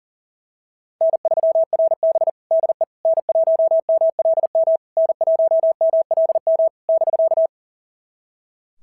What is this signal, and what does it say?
Morse code, heard as a keyed tone, about 41 wpm, 690 Hz. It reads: N3RB DE N1MLM N1MLM <BK>
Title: SOTA QSO Practice 40wpm